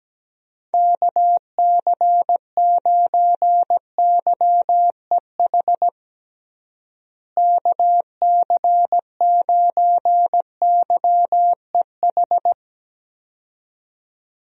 Morse code, heard as a keyed tone, about 17 wpm, 700 Hz. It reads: KC9YEH KC9YEH
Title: USA Call Signs - 6 Characters - Repeated 2x - 17wpm